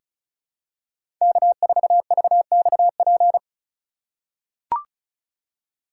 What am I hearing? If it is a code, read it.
Morse code, heard as a keyed tone, about 36 wpm, 700 Hz. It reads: K4VXP
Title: USA Call Signs - 5 Characters - 2x Character Spacing - 35wpm